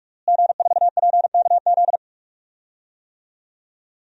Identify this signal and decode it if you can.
Morse code, heard as a keyed tone, about 46 wpm, 700 Hz. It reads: G4PKZ
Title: DXCC-100 Call Signs - 5 Characters - Repeated 3x - 45wpm